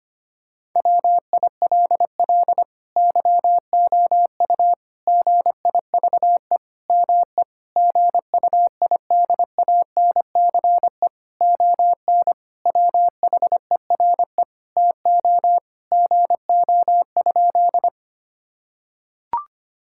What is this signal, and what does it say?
Morse code, heard as a keyed tone, about 25 wpm, 700 Hz. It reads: WILL YOU GIVE ME GUIDANCE ON WHERE TO GO?
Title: Sentences from Top 1600 Words 25wpm